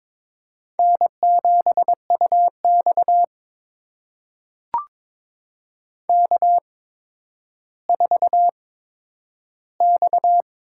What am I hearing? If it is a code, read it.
Morse code, heard as a keyed tone, about 22 wpm, spaced as for 11 wpm, 700 Hz. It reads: N7UX K4X
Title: USA Call Signs - 4 Characters - 8x Character Spacing - 22wpm